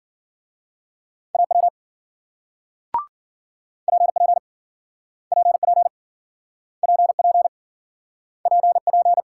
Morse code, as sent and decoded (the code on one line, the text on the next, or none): .- .-- / .--. .--. / .--. .--. / .--. .--. / .--. .--.
AW PP PP PP PP